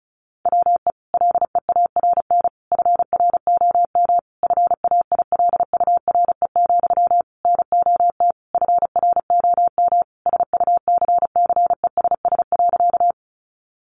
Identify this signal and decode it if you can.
Morse code, heard as a keyed tone, about 34 wpm, 700 Hz. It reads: WE LEARN FROM FAILURE, NOT FROM SUCCESS.
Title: Sentences from Top 900 Words 35wpm